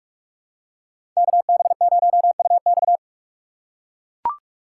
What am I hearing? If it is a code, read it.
Morse code, heard as a keyed tone, about 42 wpm, 700 Hz. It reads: KB0UX
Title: USA Call Signs - 5 Characters - 2x Character Spacing - 45wpm